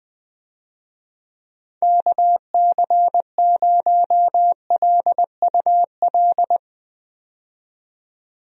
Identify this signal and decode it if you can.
Morse code, heard as a keyed tone, about 20 wpm, 700 Hz. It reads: KC0LUL
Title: USA Call Signs - 6 Characters - 20wpm